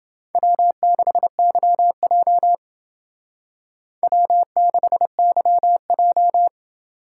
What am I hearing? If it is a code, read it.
Morse code, heard as a keyed tone, about 29 wpm, 700 Hz. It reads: W6YJ W6YJ